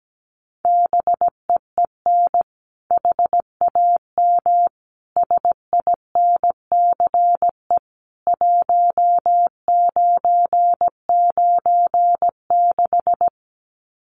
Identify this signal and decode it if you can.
Morse code, heard as a keyed tone, about 17 wpm, 700 Hz. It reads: BEEN HAM SINCE 1996
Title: Common QSO Phrases 17wpm